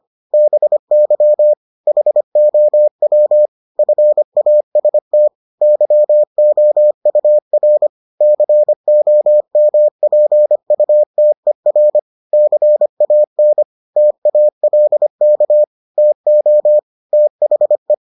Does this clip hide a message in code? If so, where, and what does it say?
Yes: Morse code from 0.3 to 17.9 s, heard as a keyed tone, about 25 wpm, 600 Hz. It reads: BY HOW FAST YOUR COMPUTER CAN TALK TO THE